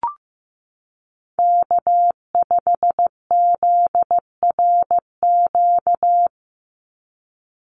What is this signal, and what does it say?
Morse code, heard as a keyed tone, about 15 wpm, 700 Hz. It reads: K5ZRQ
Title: USA Call Signs 15wpm